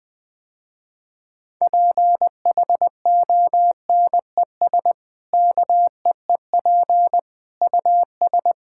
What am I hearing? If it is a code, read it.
Morse code, heard as a keyed tone, about 21 wpm, 700 Hz. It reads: PHONES KEEP US
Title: Easy Sentences with Hints - 20wpm